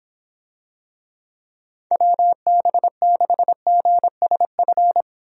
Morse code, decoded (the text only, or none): WB6GSF